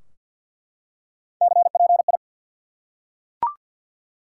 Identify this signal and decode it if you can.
Morse code, heard as a keyed tone, about 47 wpm, 700 Hz. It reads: CPI
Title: Top 180 QSO Elements 50wpm